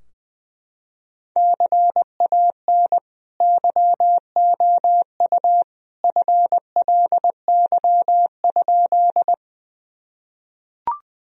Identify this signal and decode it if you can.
Morse code, heard as a keyed tone, about 20 wpm, 700 Hz. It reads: CAN YOU FLY?
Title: Sentences from Top 1000 Words - Review Part 2 - 20wpm